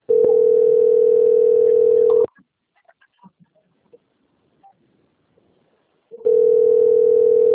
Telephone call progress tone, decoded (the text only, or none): ringback tone